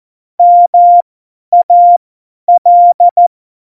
Morse code, heard as a keyed tone, about 14 wpm, spaced as for 10 wpm, 700 Hz. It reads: MAL
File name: Morsekurs / Witze / Witz 10. Oktober 17, Tempo 50